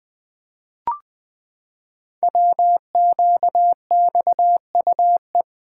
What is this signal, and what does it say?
Morse code, heard as a keyed tone, about 20 wpm, 700 Hz. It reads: WQXUE
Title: Code Groups - 5 Letters - 20wpm